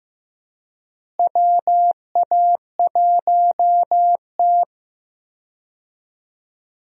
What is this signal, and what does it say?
Morse code, heard as a keyed tone, about 15 wpm, 700 Hz. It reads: WA1T
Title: USA Call Signs - 4 Characters - Repeated 3x - 15wpm